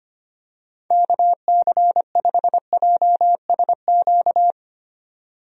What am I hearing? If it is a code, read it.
Morse code, heard as a keyed tone, about 25 wpm, 700 Hz. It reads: KC5JSQ
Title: USA Call Signs - 6 Characters - 25wpm